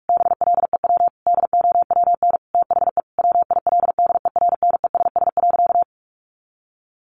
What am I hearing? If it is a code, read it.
Morse code, heard as a keyed tone, about 46 wpm, 700 Hz. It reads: BLEW DOWN THE WILDERNESS.
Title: A Sound Of Thunder 45wpm